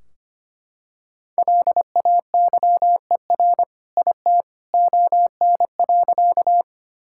Morse code, heard as a keyed tone, about 25 wpm, 700 Hz. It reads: LAYER IT ON.
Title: Sentences from Top 1600 Words 25wpm